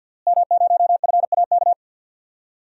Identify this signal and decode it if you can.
Morse code, heard as a keyed tone, about 51 wpm, 700 Hz. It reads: M0FAK
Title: DXCC-100 Call Signs - 5 Characters - Repeated 3x - 50wpm